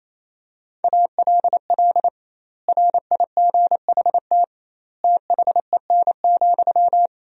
Morse code, heard as a keyed tone, about 28 wpm, 700 Hz. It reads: ALL RIGHT THEN,
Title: 2x Word-Spacing - Sentences from Top 500 Words - 28wpm